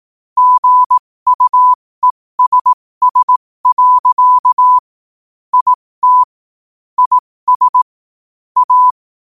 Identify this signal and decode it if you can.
Morse code, heard as a keyed tone, about 18 wpm, spaced as for 15 wpm, 1000 Hz. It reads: GUESS. IT IS A
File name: qotdfiles / 15wpm